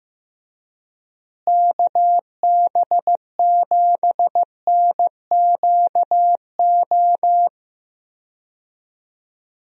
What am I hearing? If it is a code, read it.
Morse code, heard as a keyed tone, about 15 wpm, 700 Hz. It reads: KB7NQO
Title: USA Call Signs - 6 Characters - Repeated 3x - 15wpm